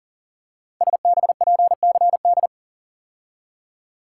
Morse code, heard as a keyed tone, about 39 wpm, 700 Hz. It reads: SBPCD
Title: Code Groups - 5 Letters - 40wpm